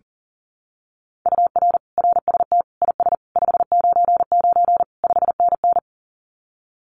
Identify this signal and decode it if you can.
Morse code, heard as a keyed tone, about 40 wpm, 700 Hz. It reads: UR RST IS 599 5NN
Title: Common QSO Phrases Encore 40wpm